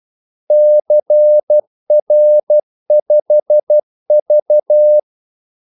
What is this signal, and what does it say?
Morse code, heard as a keyed tone, about 12 wpm, 600 Hz. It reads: CR5V
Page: Kaldesignaler 12 wpm